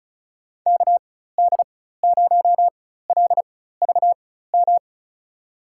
Morse code, heard as a keyed tone, about 36 wpm, spaced as for 16 wpm, 700 Hz. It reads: KD0LVM
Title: USA Call Signs - 6 Characters - 4x Character Spacing - 35wpm